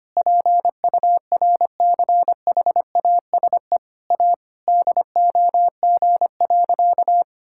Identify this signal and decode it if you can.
Morse code, heard as a keyed tone, about 25 wpm, 700 Hz. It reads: PURCHASE A DOG.
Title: Sentences from Top 1000 Words - Review Part 5 - 25wpm